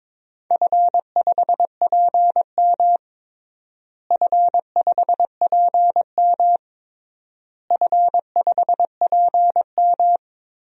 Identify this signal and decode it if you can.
Morse code, heard as a keyed tone, about 22 wpm, 700 Hz. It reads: F5PM F5PM F5PM
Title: DXCC-100 Call Signs - 4 Characters - Repeated 3x - 22wpm